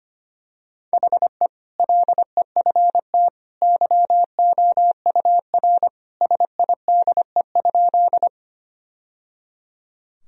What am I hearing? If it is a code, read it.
Morse code, heard as a keyed tone, about 25 wpm, 700 Hz. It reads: HE LEFT YOUR SIDE?
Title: Sentences from Top 1000 Words - Review Part 1 - 25wpm